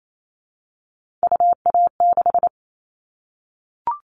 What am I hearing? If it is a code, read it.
Morse code, heard as a keyed tone, about 28 wpm, 700 Hz. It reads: UA6
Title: Callsign Prefixes for Top 100 DXCC - 28wpm